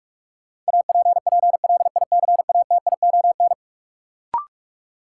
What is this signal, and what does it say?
Morse code, heard as a keyed tone, about 46 wpm, 700 Hz. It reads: APPLICATION
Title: Sentences from Top 1200 Words - New Word Drill Alt - 45wpm